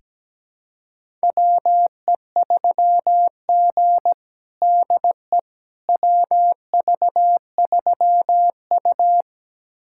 Morse code, heard as a keyed tone, about 17 wpm, 700 Hz. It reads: WE3G DE WV3U
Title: USA Callsign DE Callsign - 4 Characters - 17wpm